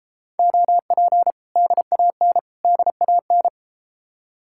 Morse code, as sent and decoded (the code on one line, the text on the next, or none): --- .--. / -.. .- -. / -.. .- -.
OP DAN DAN